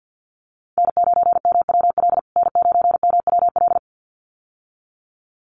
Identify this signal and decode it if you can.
Morse code, heard as a keyed tone, about 49 wpm, 700 Hz. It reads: N9MWL N9MWL
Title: USA Call Signs Repeated 2x 50wpm